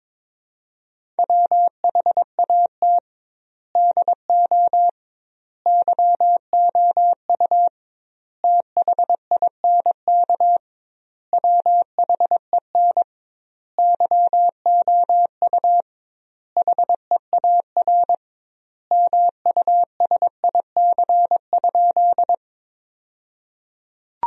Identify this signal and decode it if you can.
Morse code, heard as a keyed tone, about 22 wpm, 700 Hz. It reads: WHAT DO YOU THINK WHEN YOU HEAR MUSIC?
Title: 2x Word-Spacing - Sentences from Top 300 Words - 22wpm